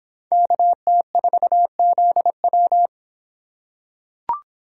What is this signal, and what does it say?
Morse code, heard as a keyed tone, about 26 wpm, 700 Hz. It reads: KT4ZW